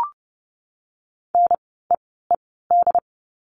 Morse code, heard as a keyed tone, about 30 wpm, spaced as for 17 wpm, 700 Hz. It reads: NEED
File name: course / 30:10wpm / 237 Lesson 38f - Letter q - Words 30wpm